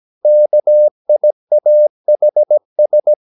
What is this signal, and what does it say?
Morse code, heard as a keyed tone, about 17 wpm, 600 Hz. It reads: KIAHS
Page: Romanen Never af Ken Follet på engelsk.